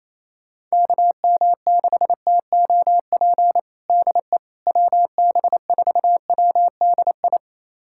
Morse code, heard as a keyed tone, about 27 wpm, 700 Hz. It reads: KM6TOP DE WB4WDI
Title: USA Callsign DE Callsign - 6 Characters - 28wpm